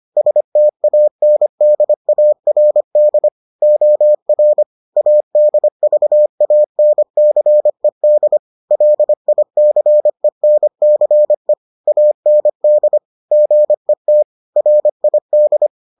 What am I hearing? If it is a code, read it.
Morse code, heard as a keyed tone, about 25 wpm, 600 Hz. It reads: STANDARD OR ADVANCED LICENCE AND GET RID